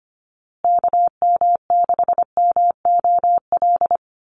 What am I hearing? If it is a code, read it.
Morse code, heard as a keyed tone, about 25 wpm, 700 Hz. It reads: KM6MOL
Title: USA Call Signs Repeated 2x 25wpm